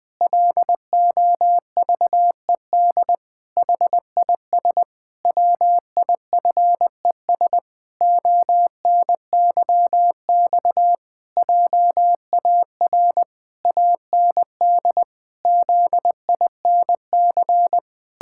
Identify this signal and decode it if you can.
Morse code, heard as a keyed tone, about 20 wpm, 700 Hz. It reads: LOVED HIS WIFES ONYX JAR AND ZINC
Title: Pangram Challenge v2 - 20wpm